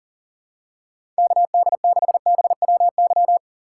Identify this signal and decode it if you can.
Morse code, heard as a keyed tone, about 37 wpm, 700 Hz. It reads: KD6BWY
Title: USA Call Signs - 6 Characters - 3x Character Spacing - 40wpm